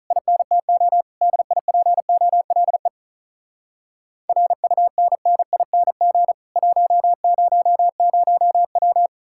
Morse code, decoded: INTO DIPOLE RUNNING 100W

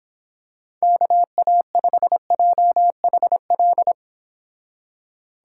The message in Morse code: KA5JHL

